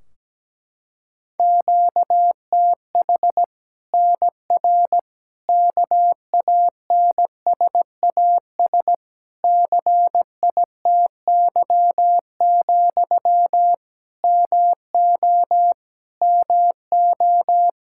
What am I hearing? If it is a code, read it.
Morse code, heard as a keyed tone, about 17 wpm, 700 Hz. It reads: QTH NR KANSAS CITY, MO MO
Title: Practice QSO - Exchange 1 - Beginning - 17wpm